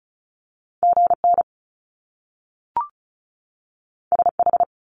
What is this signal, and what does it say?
Morse code, heard as a keyed tone, about 36 wpm, 700 Hz. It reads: GN SH
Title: Top Two Letter Combinations in Words 35wpm